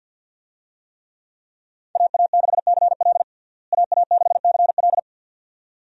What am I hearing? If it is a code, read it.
Morse code, heard as a keyed tone, about 50 wpm, 700 Hz. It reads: AA6CL AA6CL
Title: USA Call Signs - 5 Characters - Repeated 2x - 50wpm